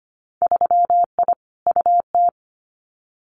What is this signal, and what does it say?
Morse code, heard as a keyed tone, about 25 wpm, 700 Hz. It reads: 3I UT
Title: Winter Field Day Exchanges Encore 25wpm